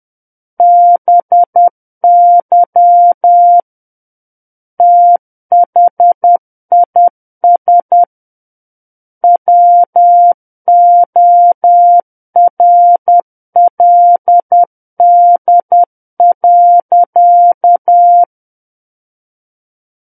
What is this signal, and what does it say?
Morse code, heard as a keyed tone, about 10 wpm, 700 Hz. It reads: BY THIS WORLD.
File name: Book One / 01-Chapter 1 - 10 WPM